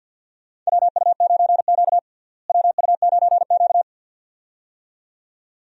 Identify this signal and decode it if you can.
Morse code, heard as a keyed tone, about 50 wpm, 700 Hz. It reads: WU9Q WU9Q
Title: USA Call Signs - 4 Characters - Repeated 2x - 50wpm